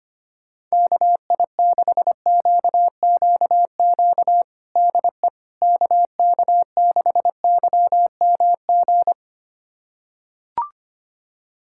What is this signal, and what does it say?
Morse code, heard as a keyed tone, about 25 wpm, 700 Hz. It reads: KI6QQQ DE KK6YMG
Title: USA Callsign DE Callsign 25wpm